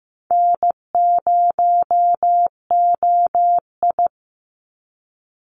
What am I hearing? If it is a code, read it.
Morse code, heard as a keyed tone, about 15 wpm, 700 Hz. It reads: N0OI
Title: NA SOTA Activator Callsigns 15wpm